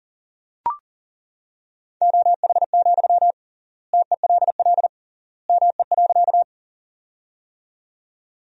Morse code, heard as a keyed tone, about 40 wpm, 700 Hz. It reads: OH, TELL ME.
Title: 3x Word-Spacing - Sentences from Top 500 Words - 40wpm